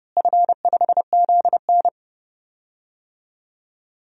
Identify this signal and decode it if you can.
Morse code, heard as a keyed tone, about 30 wpm, 700 Hz. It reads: F5ZN
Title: DXCC-100 Call Signs - 4 Characters - Repeated 3x - 30wpm